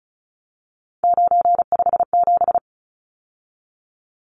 Morse code, heard as a keyed tone, about 34 wpm, 700 Hz. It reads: 957
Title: Numbers - 3-Digit 35wpm